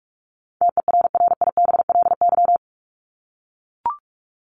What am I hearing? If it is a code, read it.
Morse code, heard as a keyed tone, about 44 wpm, 700 Hz. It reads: TERRIBLY
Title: Top 1751-2000 Words 45wpm